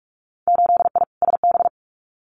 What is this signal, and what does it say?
Morse code, heard as a keyed tone, about 46 wpm, 700 Hz. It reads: 8I SB
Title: Winter Field Day Exchanges Encore 45wpm